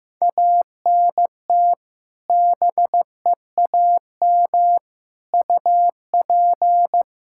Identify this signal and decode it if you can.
Morse code, heard as a keyed tone, about 15 wpm, 700 Hz. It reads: ANT BEAM UP
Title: Practice QSO - Exchange 2 - Info - 15wpm